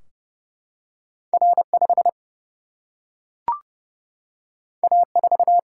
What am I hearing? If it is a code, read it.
Morse code, heard as a keyed tone, about 30 wpm, 700 Hz. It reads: R5 A4
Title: Single-Letter Single-Number 30wpm